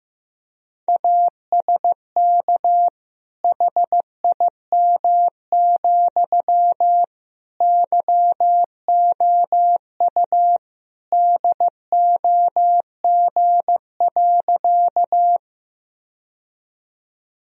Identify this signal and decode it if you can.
Morse code, heard as a keyed tone, about 15 wpm, 700 Hz. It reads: ASK HIM, YOU DOG.
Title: Sentences from Top 1000 Words - Review Part 2 - 15wpm